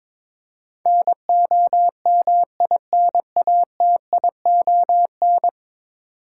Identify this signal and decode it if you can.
Morse code, heard as a keyed tone, about 22 wpm, 700 Hz. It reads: NOMINATION
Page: morse-code-ninja-practice-22wpm